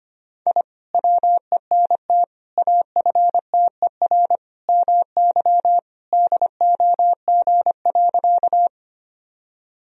Morse code, heard as a keyed tone, about 25 wpm, 700 Hz. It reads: I WENT AFTER MY DOG.
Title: Sentences from Top 1000 Words - Review Part 2 - 25wpm